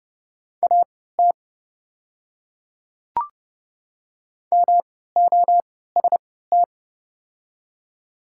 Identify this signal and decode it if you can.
Morse code, heard as a keyed tone, about 29 wpm, spaced as for 17 wpm, 690 Hz. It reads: AT MOST
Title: 3x Character Spacing - Top 200 Words - 30wpm